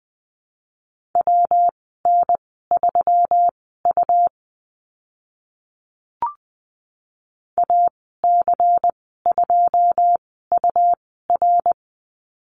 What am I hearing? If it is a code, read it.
Morse code, heard as a keyed tone, about 20 wpm, spaced as for 14 wpm, 700 Hz. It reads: WN3U AC2UR
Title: 128 Lesson 21g - Number 6 - Call Signs 20wpm